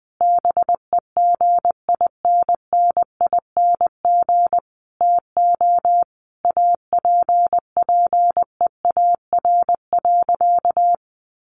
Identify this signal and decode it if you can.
Morse code, heard as a keyed tone, about 20 wpm, 700 Hz. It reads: BEGINNING TO APPEAR.
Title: Sentences from Top 1200 Words 20wpm